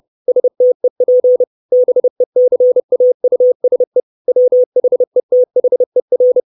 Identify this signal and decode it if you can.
Morse code, heard as a keyed tone, about 30 wpm, 500 Hz. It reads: STEP BECAUSE WHETHER
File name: 632 Words 30WPM